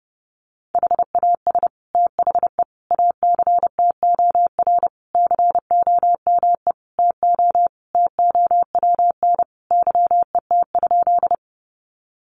Morse code, heard as a keyed tone, about 30 wpm, 700 Hz. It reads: HAS THE ACTOR COME TO TOWN YET?
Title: Sentences from Top 1500 Words 30wpm